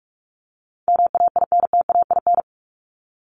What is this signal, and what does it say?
Morse code, heard as a keyed tone, about 44 wpm, 700 Hz. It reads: MAINTAIN
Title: Top 501-750 Words 45wpm